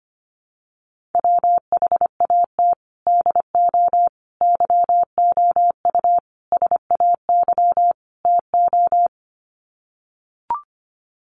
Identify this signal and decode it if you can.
Morse code, heard as a keyed tone, about 25 wpm, 700 Hz. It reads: WHAT DO YOU SAY TO
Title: Sets of 5 Words 25wpm